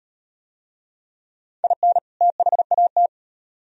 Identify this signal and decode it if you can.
Morse code, heard as a keyed tone, about 38 wpm, 700 Hz. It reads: IN THAT